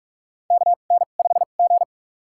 Morse code, heard as a keyed tone, about 44 wpm, spaced as for 32 wpm, 690 Hz. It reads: KN5G
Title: USA Call Signs - 4 Characters - 2x Character Spacing - 45wpm